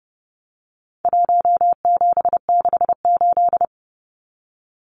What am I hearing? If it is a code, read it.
Morse code, heard as a keyed tone, about 30 wpm, 700 Hz. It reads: 1768
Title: Numbers - Contest Style - 4-Digit 30wpm